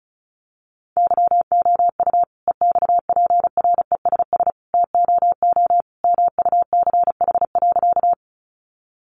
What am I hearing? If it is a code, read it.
Morse code, heard as a keyed tone, about 34 wpm, 700 Hz. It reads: YOU EXPRESS TOO MUCH.
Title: Sentences from Top 1100 Words 35wpm